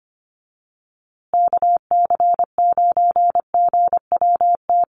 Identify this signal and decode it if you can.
Morse code, heard as a keyed tone, about 25 wpm, 700 Hz. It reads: KC9GWT
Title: USA Call Signs Repeated 3x 25wpm